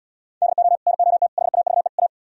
Morse code, heard as a keyed tone, about 77 wpm, 700 Hz. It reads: DO NOT HAVE A